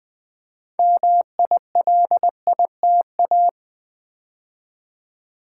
Morse code, heard as a keyed tone, about 20 wpm, 700 Hz. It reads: MILITA
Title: Build A Word - 3 Letters At A Time - 8-Letter Words - 20wpm